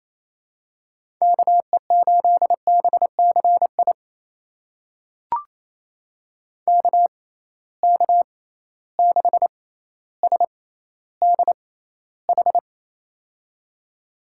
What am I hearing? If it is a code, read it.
Morse code, heard as a keyed tone, about 28 wpm, spaced as for 14 wpm, 700 Hz. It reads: KE8BCI KK6SDH
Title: USA Call Signs - 6 Characters - 6x Character Spacing - 28wpm